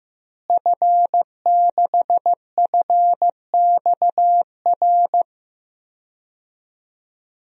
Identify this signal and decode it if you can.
Morse code, heard as a keyed tone, about 15 wpm, 700 Hz. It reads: F6FXR
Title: DXCC-100 Call Signs - 5 Characters - Repeated 2x - 15wpm